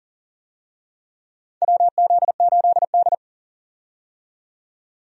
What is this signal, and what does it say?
Morse code, heard as a keyed tone, about 38 wpm, 690 Hz. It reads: WZ8D
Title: USA Call Signs - 4 Characters - 40wpm